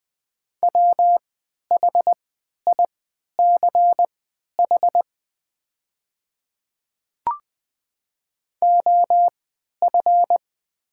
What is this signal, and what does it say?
Morse code, heard as a keyed tone, about 20 wpm, spaced as for 11 wpm, 700 Hz. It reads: WHICH OF
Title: 3x Character Spacing - Top 100 Words - 20wpm